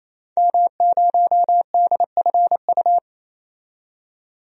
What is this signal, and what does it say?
Morse code, heard as a keyed tone, about 27 wpm, 700 Hz. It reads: M0DFU
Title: DXCC-100 Call Signs - 5 Characters - 28wpm